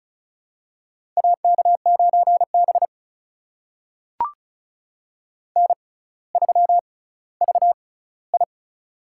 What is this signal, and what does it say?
Morse code, heard as a keyed tone, about 36 wpm, spaced as for 18 wpm, 700 Hz. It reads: AK9B N3VI
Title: USA Call Signs - 4 Characters - 6x Character Spacing - 35wpm